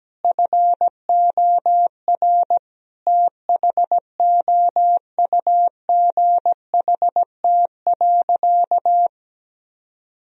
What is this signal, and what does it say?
Morse code, heard as a keyed tone, about 17 wpm, 700 Hz. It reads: FOR THOUGHT.
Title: Sentences from Top 1000 Words - Review Part 4 - 17wpm